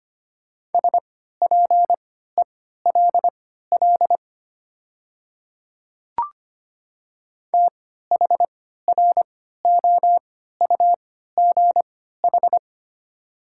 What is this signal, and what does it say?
Morse code, heard as a keyed tone, about 25 wpm, spaced as for 14 wpm, 700 Hz. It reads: SPELL THROUGH
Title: 3x Character Spacing - Top 300 Words - 25wpm